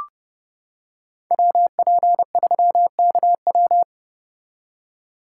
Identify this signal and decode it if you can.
Morse code, heard as a keyed tone, about 30 wpm, 700 Hz. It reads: WP3KW